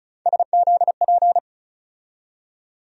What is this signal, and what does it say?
Morse code, heard as a keyed tone, about 36 wpm, 700 Hz. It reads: SZP
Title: USA Call Sign Postfixes - 3 Letters 35wpm